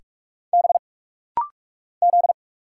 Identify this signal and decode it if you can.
Morse code, heard as a keyed tone, about 46 wpm, 700 Hz. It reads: B Z
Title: Single Letters - Rapid-Fire 45wpm